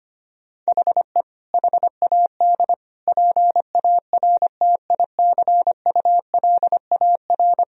Morse code, heard as a keyed tone, about 25 wpm, 700 Hz. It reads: HE HAD PARTICULAR
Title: Sentences from Top 1000 Words - Review Part 4 - 25wpm